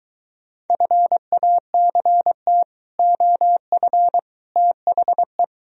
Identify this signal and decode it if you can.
Morse code, heard as a keyed tone, about 23 wpm, 700 Hz. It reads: FACT OF THE